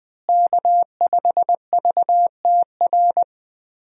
Morse code, heard as a keyed tone, about 21 wpm, 700 Hz. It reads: K5VTR
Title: USA Call Signs - 5 Characters - Repeated 2x - 20wpm